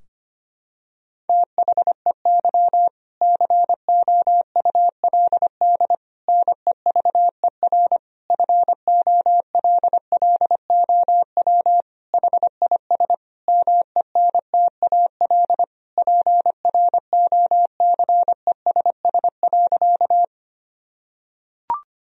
Morse code, heard as a keyed tone, about 25 wpm, 700 Hz. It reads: THEY COULD NEVER FOLLOW HIS MENTAL PROCESS.
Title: Sentences from Top 1000 Words - Review Part 4 - 25wpm